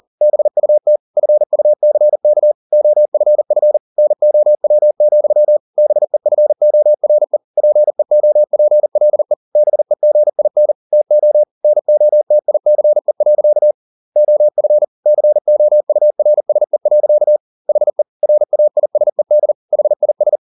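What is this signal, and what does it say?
Morse code, heard as a keyed tone, about 40 wpm, 600 Hz. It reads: BUT FUCK OFF NOW, BEFORE PEOPLE BEGIN TO NOTICE. OF COURSE. HE RAISED HIS
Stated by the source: Morse code at 40 wpm, practice text, CW med Gnister.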